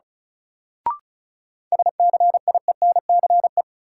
Morse code, heard as a keyed tone, about 36 wpm, 700 Hz. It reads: SCIENCE
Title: Sentences from Top 1000 Words - Review Part 2 - New Word Drill - 35wpm